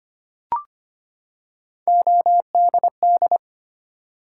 Morse code, heard as a keyed tone, about 25 wpm, 700 Hz. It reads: ODD
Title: Three Letter Words - 25wpm